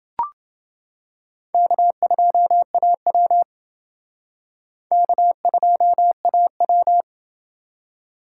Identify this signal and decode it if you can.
Morse code, heard as a keyed tone, about 28 wpm, 700 Hz. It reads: K2AW K2AW